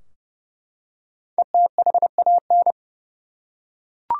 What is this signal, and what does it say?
Morse code, heard as a keyed tone, about 30 wpm, 700 Hz. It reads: ETHAN
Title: USA Common Names - Max 5 Letters - 30wpm